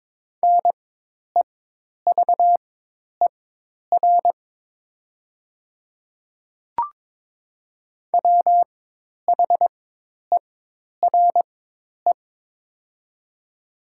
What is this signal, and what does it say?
Morse code, heard as a keyed tone, about 22 wpm, spaced as for 10 wpm, 700 Hz. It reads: NEVER WHERE
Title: 4x Character Spacing - Top 300 Words - 22wpm